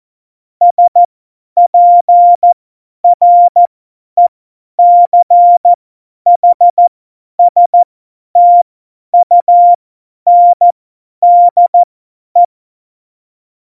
Morse code, heard as a keyed tone, about 14 wpm, spaced as for 10 wpm, 700 Hz. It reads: SPRECHSTUNDE